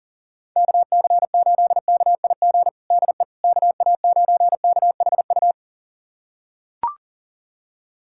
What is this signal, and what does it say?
Morse code, heard as a keyed tone, about 40 wpm, 700 Hz. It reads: KC8KIG DE KA9KHU
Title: USA Callsign DE Callsign - 6 Characters - 40wpm